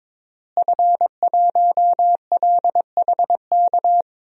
Morse code, heard as a keyed tone, about 22 wpm, 700 Hz. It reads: F1LHK
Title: DXCC-100 Call Signs - 5 Characters - Repeated 2x - 22wpm